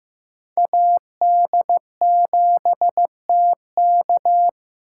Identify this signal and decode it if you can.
Morse code, heard as a keyed tone, about 15 wpm, 700 Hz. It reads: AD7TK
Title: USA Call Signs - 2x Character Spacing - 15wpm